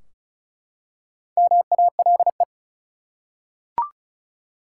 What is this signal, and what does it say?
Morse code, heard as a keyed tone, about 36 wpm, 700 Hz. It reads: MALE
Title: Four Letter Words - 35wpm